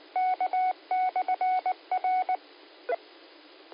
Morse code, heard as a keyed tone, about 19 wpm, 740 Hz. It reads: K/R
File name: qsorec SimplexLogic